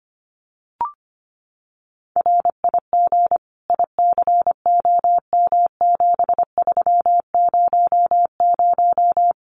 Morse code, heard as a keyed tone, about 25 wpm, 700 Hz. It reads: RIG ICOM7300
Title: Common QSO Phrases 25wpm